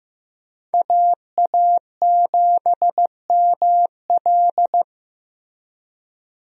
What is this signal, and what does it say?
Morse code, heard as a keyed tone, about 15 wpm, 700 Hz. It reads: AA7ML
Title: USA Call Signs - 5 Characters - Repeated 3x - 15wpm